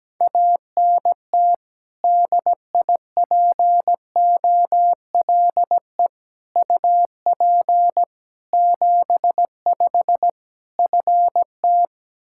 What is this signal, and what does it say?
Morse code, heard as a keyed tone, about 17 wpm, 700 Hz. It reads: ANT DIPOLE UP 75 FT
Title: K1USN SST QSO Practice 17wpm